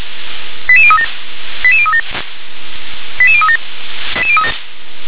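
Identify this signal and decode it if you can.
Five-tone selective call (ZVEI), heard as quick runs of five tones, tones of about 70 ms. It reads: 88037 88037 88037
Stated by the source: ZVEI five-tone call 88036 received with an SDR.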